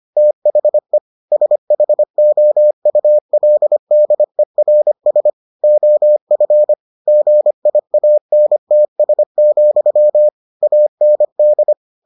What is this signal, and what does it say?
Morse code, heard as a keyed tone, about 25 wpm, 600 Hz. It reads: THE SHOULDERS OF GIANTS, AND